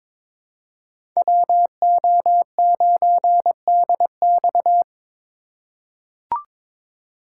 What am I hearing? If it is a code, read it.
Morse code, heard as a keyed tone, about 22 wpm, 700 Hz. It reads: WO9DX
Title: USA Call Signs - 5 Characters - 2x Character Spacing - 22wpm